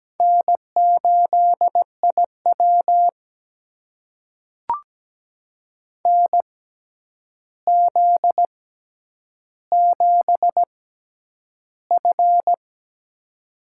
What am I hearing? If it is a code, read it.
Morse code, heard as a keyed tone, about 17 wpm, spaced as for 9 wpm, 700 Hz. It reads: N8IW NZ7F
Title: USA Call Signs - 4 Characters - 6x Character Spacing - 17wpm